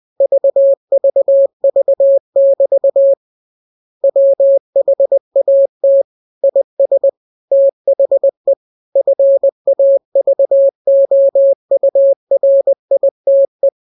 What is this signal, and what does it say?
Morse code, heard as a keyed tone, about 20 wpm, 550 Hz. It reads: VVV<BT> WHAT IS THE FAVOURITE
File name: jokes / joke 13 20 0000